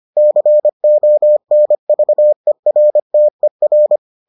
Morse code, heard as a keyed tone, about 25 wpm, 600 Hz. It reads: CONVERTER